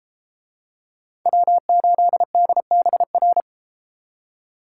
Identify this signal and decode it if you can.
Morse code, heard as a keyed tone, about 32 wpm, 700 Hz. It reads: W8DBR